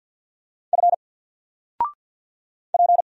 Morse code, heard as a keyed tone, about 50 wpm, 700 Hz. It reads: F P
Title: Single Letter-Number 50wpm